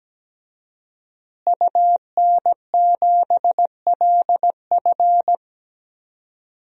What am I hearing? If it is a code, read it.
Morse code, heard as a keyed tone, about 17 wpm, 700 Hz. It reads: UN7LF
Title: DXCC-100 Call Signs - 5 Characters - Repeated 3x - 17wpm